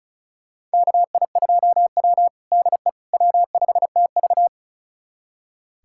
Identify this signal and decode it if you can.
Morse code, heard as a keyed tone, about 36 wpm, 700 Hz. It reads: KI2W DE W5TV
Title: USA Callsign DE Callsign - 4 Characters - 35wpm